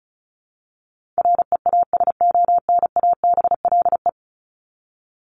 Morse code, heard as a keyed tone, about 34 wpm, 700 Hz. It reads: REASONABLE
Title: Top 1501-1750 Words 35wpm